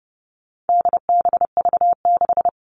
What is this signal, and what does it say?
Morse code, heard as a keyed tone, about 30 wpm, 700 Hz. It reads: DBV6
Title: All Cut Numbers - 4-Digit 30wpm